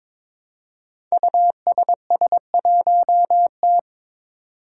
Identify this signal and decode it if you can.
Morse code, heard as a keyed tone, about 22 wpm, 700 Hz. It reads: USS1T
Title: Five Challenging Letter-Number Combinations 22wpm